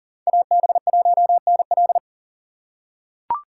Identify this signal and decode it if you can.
Morse code, heard as a keyed tone, about 40 wpm, 700 Hz. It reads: AB1NL
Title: USA Call Signs - 5 Characters - 6x Character Spacing - 40wpm